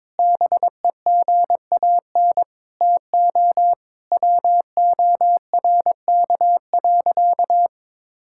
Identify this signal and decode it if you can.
Morse code, heard as a keyed tone, about 22 wpm, 700 Hz. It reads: BEGAN TO WORK.
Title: Sentences from Top 1000 Words - Review Part 6 - 22wpm